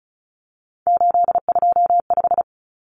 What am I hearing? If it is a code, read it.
Morse code, heard as a keyed tone, about 34 wpm, 700 Hz. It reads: 825
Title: Common Cut Numbers - 3-Digit 35wpm